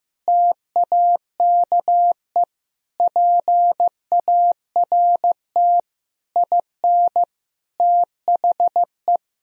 Morse code, heard as a keyed tone, about 15 wpm, 700 Hz. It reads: TAKE PART IN THE
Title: Sentences from Top 1000 Words - Review Part 4 - 15wpm